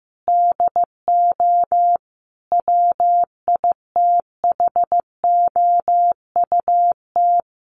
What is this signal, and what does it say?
Morse code, heard as a keyed tone, about 15 wpm, 700 Hz. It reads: DO WITHOUT
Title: Sentences from Top 600 Words 15wpm